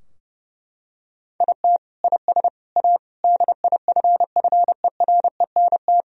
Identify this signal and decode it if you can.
Morse code, heard as a keyed tone, about 30 wpm, 700 Hz. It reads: IT IS A DIFFERENT
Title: Sentences from Top 1000 Words - Review Part 3 - 30wpm